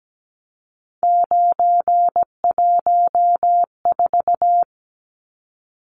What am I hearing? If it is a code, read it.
Morse code, heard as a keyed tone, about 17 wpm, 700 Hz. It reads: 914
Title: Numbers - 3-Digit 17wpm